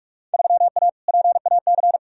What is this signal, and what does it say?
Morse code, heard as a keyed tone, about 41 wpm, 700 Hz. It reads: 3A PAC
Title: Field Day Exchanges - 45wpm